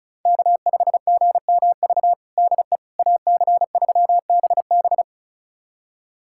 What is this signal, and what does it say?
Morse code, heard as a keyed tone, about 36 wpm, 700 Hz. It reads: K5GMV DE AC3BB
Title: USA Callsign DE Callsign - 5 Characters - 35wpm